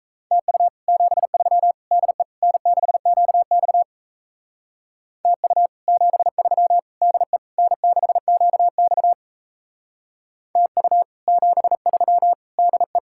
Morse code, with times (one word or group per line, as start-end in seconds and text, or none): 0.3-0.7 TU
0.9-1.7 73
1.9-2.2 DE
2.4-3.8 N6QX
5.2-5.7 TU
5.9-6.8 73
7.0-7.4 DE
7.6-9.1 N6QX
10.6-11.0 TU
11.3-12.3 73
12.6-13.0 DE